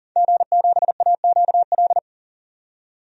Morse code, heard as a keyed tone, about 40 wpm, 700 Hz. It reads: G7AQL
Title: DXCC-100 Call Signs - 5 Characters - 40wpm